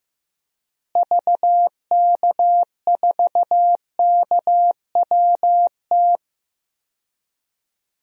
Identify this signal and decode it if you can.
Morse code, heard as a keyed tone, about 15 wpm, 700 Hz. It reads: VK4KWT
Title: DXCC-100 Call Signs - 6 Characters - Repeated 2x - 15wpm